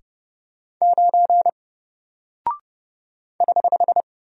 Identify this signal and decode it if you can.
Morse code, heard as a keyed tone, about 30 wpm, 700 Hz. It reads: 9 <HH>
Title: Single Character (Letter:Number:Punctuation:Prosign) 30wpm